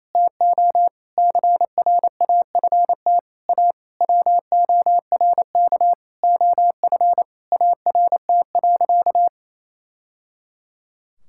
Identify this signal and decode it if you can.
Morse code, heard as a keyed tone, about 28 wpm, 690 Hz. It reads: TO CRAFT A WORK OF ART.
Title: Sentences from Top 1000 Words - Review Part 6 - 28wpm